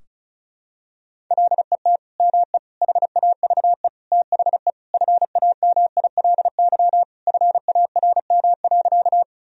Morse code, heard as a keyed tone, about 34 wpm, 700 Hz. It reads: LET ME HAVE THE FAMILY FARM.
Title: Sentences from Top 1000 Words - Review Part 2 - 35wpm